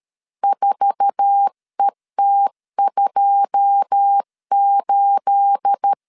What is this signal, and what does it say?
Morse code, heard as a keyed tone, about 13 wpm, 800 Hz. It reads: 4ET28
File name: LEZIONI / 9 28 12wpm 2351